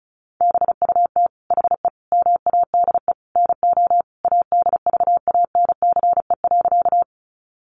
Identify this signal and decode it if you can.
Morse code, heard as a keyed tone, about 34 wpm, 700 Hz. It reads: BUT HE MADE NO ADVANCE.
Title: Sentences from Top 1400 Words 35wpm